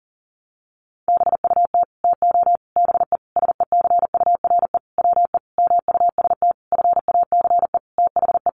Morse code, heard as a keyed tone, about 40 wpm, 700 Hz. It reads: BUT TO BE SECURE WE MUST FACE THE
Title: Sentences from Top 1200 Words 40wpm